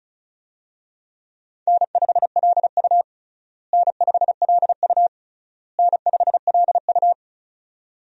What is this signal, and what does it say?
Morse code, heard as a keyed tone, about 34 wpm, 700 Hz. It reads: N5LU N5LU N5LU
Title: USA Call Signs - 4 Characters - Repeated 3x - 35wpm